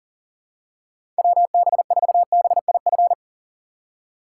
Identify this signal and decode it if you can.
Morse code, heard as a keyed tone, about 40 wpm, 700 Hz. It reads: WB4BIF
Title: USA Call Signs - 6 Characters - 4x Character Spacing - 40wpm